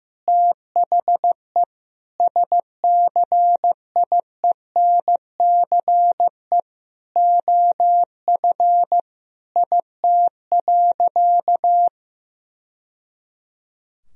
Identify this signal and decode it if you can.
Morse code, heard as a keyed tone, about 15 wpm, 700 Hz. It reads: THE SCIENCE OF IT.
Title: Sentences from Top 1000 Words - Review Part 2 - 15wpm